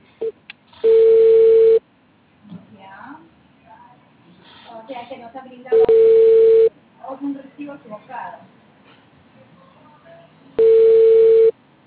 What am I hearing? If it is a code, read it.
Ringback tone (China, 450 Hz), with speech.